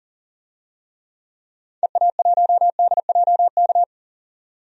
Morse code, heard as a keyed tone, about 40 wpm, 700 Hz. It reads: EA1DJK
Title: DXCC-100 Call Signs - 6 Characters - 40wpm